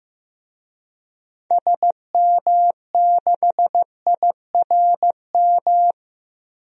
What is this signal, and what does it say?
Morse code, heard as a keyed tone, about 16 wpm, 700 Hz. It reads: SM6IRM
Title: DXCC-100 Call Signs - 6 Characters - 15wpm